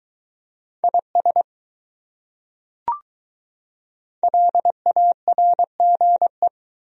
Morse code, heard as a keyed tone, about 20 wpm, 700 Hz. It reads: IS LARGE